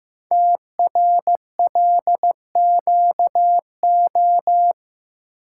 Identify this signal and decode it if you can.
Morse code, heard as a keyed tone, about 15 wpm, 700 Hz. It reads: TRLQO
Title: Code Groups - 5 Letter:Numbers - 15wpm